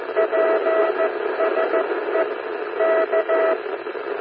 Morse code, heard as a keyed tone, about 14 wpm, 650 Hz, with noise. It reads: PSE K
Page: CW na FM